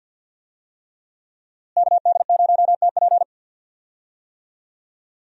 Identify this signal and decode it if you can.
Morse code, heard as a keyed tone, about 47 wpm, 700 Hz. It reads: KD0TP
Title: USA Call Signs - 5 Characters - Repeated 3x - 50wpm